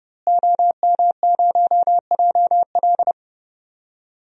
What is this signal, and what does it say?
Morse code, heard as a keyed tone, about 32 wpm, 700 Hz. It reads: OM0JL
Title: DXCC-100 Call Signs - 5 Characters - Repeated 3x - 30wpm